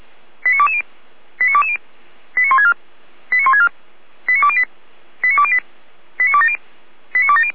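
Five-tone selective call (ZVEI), heard as quick runs of five tones, tones of about 80 ms. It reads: 89220 89220 89164 89164 89208 89208 89270 89270
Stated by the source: ZVEI five-tone selective call received by SDR, fire brigade Feuerwehr Orlamünde.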